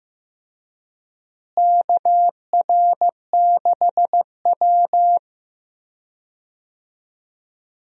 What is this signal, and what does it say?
Morse code, heard as a keyed tone, about 16 wpm, 700 Hz. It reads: KR6W
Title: USA Call Signs - 4 Characters - Repeated 2x - 15wpm